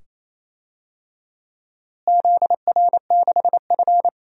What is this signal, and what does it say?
Morse code, heard as a keyed tone, about 27 wpm, 700 Hz. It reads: ZR6F
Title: DXCC-100 Call Signs - 4 Characters - Repeated 2x - 28wpm